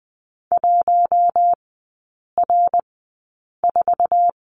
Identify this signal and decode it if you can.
Morse code, heard as a keyed tone, about 20 wpm, spaced as for 8 wpm, 700 Hz. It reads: 1R4
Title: 053 Lesson 10e - Letter r - Three Characters 20wpm